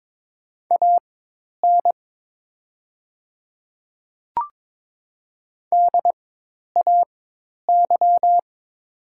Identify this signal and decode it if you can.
Morse code, heard as a keyed tone, about 22 wpm, spaced as for 10 wpm, 700 Hz. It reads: AN DAY
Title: 4x Character Spacing - Top 100 Words - 22wpm